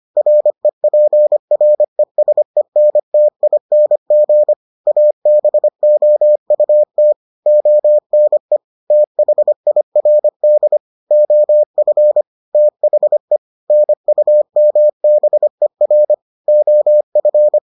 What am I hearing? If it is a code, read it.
Morse code, heard as a keyed tone, about 25 wpm, 600 Hz. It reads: REPRESENTING ABOUT ONE THIRD OF THE NUMBER OF